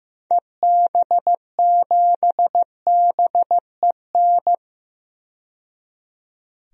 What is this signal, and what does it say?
Morse code, heard as a keyed tone, about 15 wpm, 700 Hz. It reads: EB7BEN
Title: DXCC-100 Call Signs - 6 Characters - 15wpm